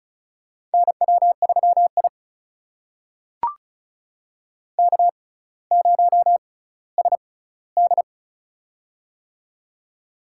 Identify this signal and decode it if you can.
Morse code, heard as a keyed tone, about 36 wpm, spaced as for 18 wpm, 700 Hz. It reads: NW3I K0SD
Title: USA Call Signs - 4 Characters - 6x Character Spacing - 35wpm